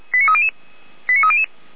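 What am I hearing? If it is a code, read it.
Five-tone selective call (ZVEI), heard as quick runs of five tones, tones of about 70 ms. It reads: 89300
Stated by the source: ZVEI five-tone selective call received via SDR.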